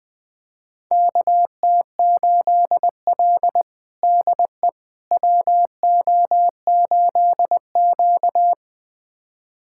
Morse code, heard as a keyed tone, about 20 wpm, 700 Hz. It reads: KT8L DE WO8Q
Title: USA Callsign DE Callsign - 4 Characters - 20wpm